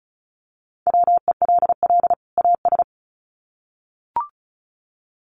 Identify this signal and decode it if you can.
Morse code, heard as a keyed tone, about 34 wpm, 700 Hz. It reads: WELL AS
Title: Sets of 2 Words 35wpm